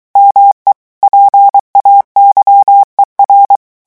Morse code, heard as a keyed tone, about 23 wpm, 800 Hz. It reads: ME PAYER
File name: morse mp3 / revision9 citations humoristiques vitesse 22 mots minute